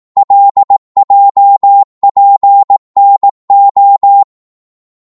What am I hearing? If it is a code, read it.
Morse code, heard as a keyed tone, about 18 wpm, 800 Hz. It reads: LJPNO
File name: Morse / ltr-18wpm